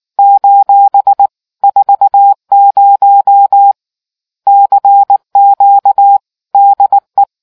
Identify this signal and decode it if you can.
Morse code, heard as a keyed tone, about 19 wpm, 800 Hz. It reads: : 40 CQ DE